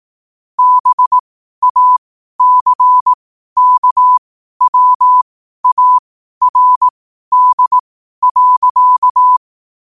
Morse code, heard as a keyed tone, about 18 wpm, spaced as for 12 wpm, 1000 Hz. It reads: BACKWARD.
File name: qotdfiles / 12wpm